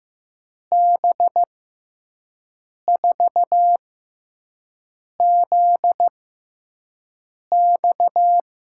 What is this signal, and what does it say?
Morse code, heard as a keyed tone, about 15 wpm, spaced as for 5 wpm, 700 Hz. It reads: B4ZX
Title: USA Call Signs - 5 Characters - 6x Character Spacing - 15wpm